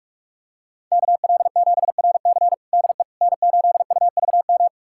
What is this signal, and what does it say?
Morse code, heard as a keyed tone, about 46 wpm, 700 Hz. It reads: KL7RC DE N8UVM
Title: USA Callsign DE Callsign - 5 Characters - 45wpm